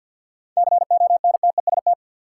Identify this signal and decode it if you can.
Morse code, heard as a keyed tone, about 52 wpm, 700 Hz. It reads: CONTEST
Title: Sentences from Top 1600 Words - New Word Drill - 50wpm